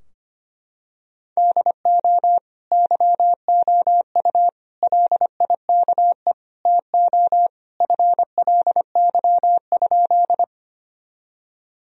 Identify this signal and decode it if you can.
Morse code, heard as a keyed tone, about 25 wpm, 700 Hz. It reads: DO YOU LIKE TO FLY?
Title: Sentences from Top 1000 Words - Review Part 2 - 25wpm